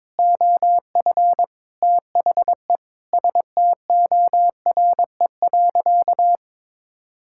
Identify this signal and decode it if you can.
Morse code, heard as a keyed tone, about 22 wpm, 700 Hz. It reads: OF THE STORE.
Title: Sentences from Top 1000 Words - Review Part 5 - 22wpm